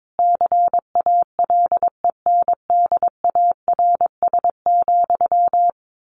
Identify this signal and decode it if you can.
Morse code, heard as a keyed tone, about 22 wpm, 700 Hz. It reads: CALENDARS,
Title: A Sound Of Thunder 22wpm